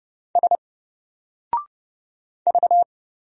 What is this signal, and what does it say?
Morse code, heard as a keyed tone, about 30 wpm, 700 Hz. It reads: S V
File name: Single-Letter-Number-30wpm